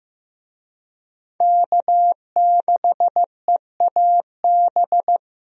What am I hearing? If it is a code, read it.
Morse code, heard as a keyed tone, about 15 wpm, 700 Hz. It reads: K6EAB
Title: USA Call Signs - 5 Characters - Repeated 2x - 15wpm